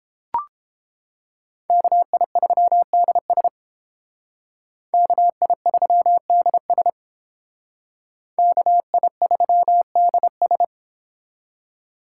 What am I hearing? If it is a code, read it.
Morse code, heard as a keyed tone, about 31 wpm, 700 Hz. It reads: KI3DS KI3DS KI3DS